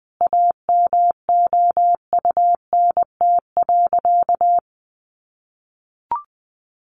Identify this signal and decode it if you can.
Morse code, heard as a keyed tone, about 20 wpm, 700 Hz. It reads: AMOUNT.
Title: Sentences from Top 700 Words 20wpm